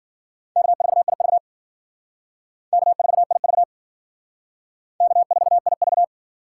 Morse code, heard as a keyed tone, about 52 wpm, 700 Hz. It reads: K4IV K4IV K4IV